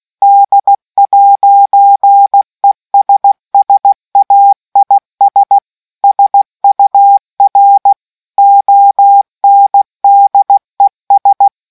Morse code, heard as a keyed tone, about 16 wpm, 800 Hz. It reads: D'ESSAIS SUR ONDES